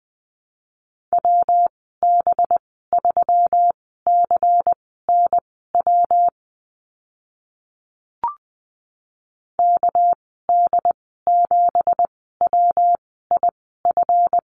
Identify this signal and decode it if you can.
Morse code, heard as a keyed tone, about 20 wpm, spaced as for 14 wpm, 700 Hz. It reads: WB3CNW KD7WIF
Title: 231 Lesson 37g - Letter x - Call Signs 20wpm